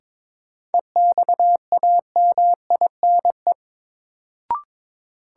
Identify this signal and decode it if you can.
Morse code, heard as a keyed tone, about 22 wpm, 700 Hz. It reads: EXAMINE
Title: Sentences from Top 1600 Words - New Word Drill Alt - 22wpm